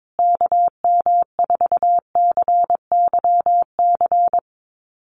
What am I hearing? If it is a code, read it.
Morse code, heard as a keyed tone, about 22 wpm, 700 Hz. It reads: KM4CYC
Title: USA Call Signs Repeated 3x 22wpm